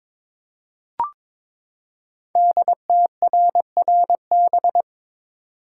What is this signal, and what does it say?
Morse code, heard as a keyed tone, about 22 wpm, 690 Hz. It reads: DTRRB
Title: Code Groups - 5 Letters - 22wpm